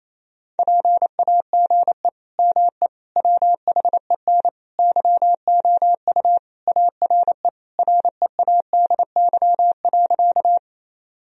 Morse code, heard as a keyed tone, about 28 wpm, 700 Hz. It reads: PAGE ME WHEN YOU ARE READY.
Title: Sentences from Top 1000 Words - Review Part 1 - 28wpm